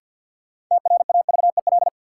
Morse code, heard as a keyed tone, about 53 wpm, 700 Hz. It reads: TRAVEL
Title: Sentences from Top 400 Words - New Word Drill - 50wpm